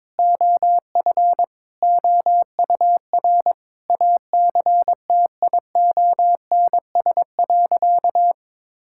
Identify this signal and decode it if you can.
Morse code, heard as a keyed tone, about 22 wpm, 700 Hz. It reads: OF OUR ACTIONS.
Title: Sentences from Top 1000 Words - Review Part 6 - 22wpm